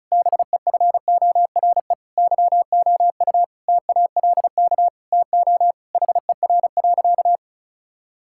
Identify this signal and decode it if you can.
Morse code, heard as a keyed tone, about 36 wpm, 690 Hz. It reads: BEFORE YOU TALK TO HER.
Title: Sentences from Top 1000 Words - Review Part 6 - 35wpm